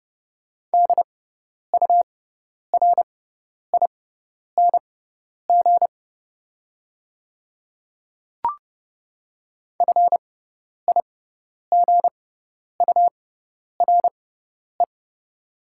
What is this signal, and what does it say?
Morse code, heard as a keyed tone, about 30 wpm, spaced as for 10 wpm, 700 Hz. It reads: DURING FIGURE
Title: 6x Character Spacing - Top 500 Words - 30wpm